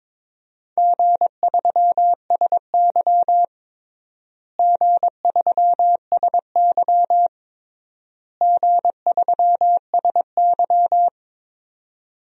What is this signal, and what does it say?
Morse code, heard as a keyed tone, about 22 wpm, 700 Hz. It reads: G3SY G3SY G3SY
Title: DXCC-100 Call Signs - 4 Characters - Repeated 3x - 22wpm